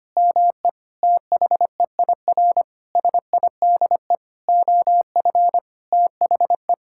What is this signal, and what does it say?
Morse code, heard as a keyed tone, about 25 wpm, 700 Hz. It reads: ME THEIR SIDE OF THE
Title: Sentences from Top 1000 Words - Review Part 5 - 25wpm